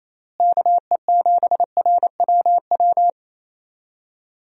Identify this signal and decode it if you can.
Morse code, heard as a keyed tone, about 27 wpm, 700 Hz. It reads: KE7RWW
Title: USA Call Signs - 6 Characters - Repeated 2x - 28wpm